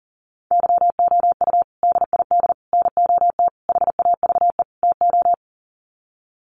Morse code, heard as a keyed tone, about 40 wpm, 700 Hz. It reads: YOU DID NOT HAVE TO
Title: Sets of 5 Words Encore 40wpm